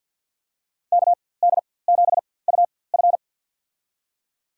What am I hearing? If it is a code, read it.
Morse code, heard as a keyed tone, about 50 wpm, spaced as for 23 wpm, 700 Hz. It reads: KD7UF
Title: USA Call Signs - 4x Character Spacing - 50wpm